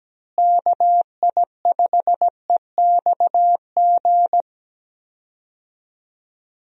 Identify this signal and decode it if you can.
Morse code, heard as a keyed tone, about 17 wpm, 700 Hz. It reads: KI5EXG
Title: USA Call Signs - 6 Characters - Repeated 2x - 17wpm